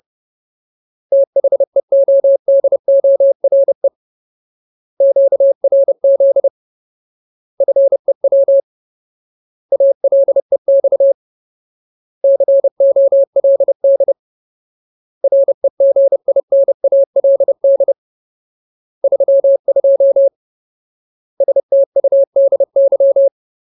Morse code, heard as a keyed tone, about 30 wpm, 550 Hz. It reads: THEODORE QRZ FEW ALEX COLD REGINALD 32 STUDY
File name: Ultimate Word List / 30WPM Ultimate